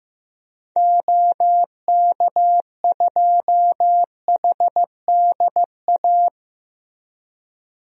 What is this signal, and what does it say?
Morse code, heard as a keyed tone, about 15 wpm, 700 Hz. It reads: OK2HDA
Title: DXCC-100 Call Signs - 6 Characters - Repeated 3x - 15wpm